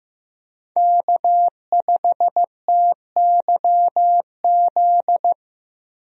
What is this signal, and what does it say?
Morse code, heard as a keyed tone, about 15 wpm, 700 Hz. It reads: K5TYZ
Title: USA Call Signs - 5 Characters - Repeated 3x - 15wpm